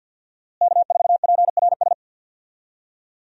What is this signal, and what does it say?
Morse code, heard as a keyed tone, about 53 wpm, 700 Hz. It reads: K4PRS
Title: USA Call Signs - 5 Characters - Repeated 3x - 50wpm